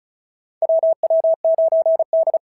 Morse code, heard as a keyed tone, about 36 wpm, 650 Hz. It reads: WW9D
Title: SOTA QSO Practice 35wpm